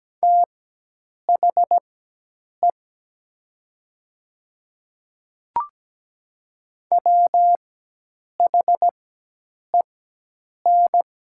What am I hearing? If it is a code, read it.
Morse code, heard as a keyed tone, about 17 wpm, spaced as for 8 wpm, 700 Hz. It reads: THE WHEN
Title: 4x Character Spacing - Top 100 Words - 17wpm